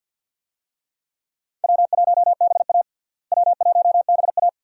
Morse code, heard as a keyed tone, about 52 wpm, 700 Hz. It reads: W1BA W1BA
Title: USA Call Signs - 4 Characters - Repeated 2x - 50wpm